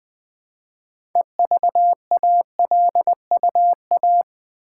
Morse code, heard as a keyed tone, about 20 wpm, 700 Hz. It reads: EVALUA
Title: Build A Word - 3 Letters At A Time - 8-Letter Words - 20wpm